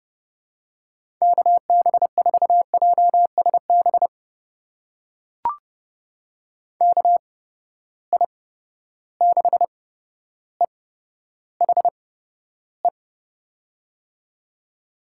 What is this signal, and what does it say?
Morse code, heard as a keyed tone, about 30 wpm, spaced as for 13 wpm, 700 Hz. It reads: KB4JSB KI6EHE
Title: USA Call Signs - 6 Characters - 8x Character Spacing - 30wpm